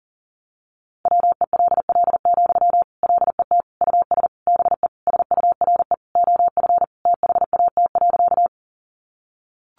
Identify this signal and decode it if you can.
Morse code, heard as a keyed tone, about 40 wpm, 700 Hz. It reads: WELL, LET US BE SURE OF THAT.
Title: Sentences from Top 300 Words 40wpm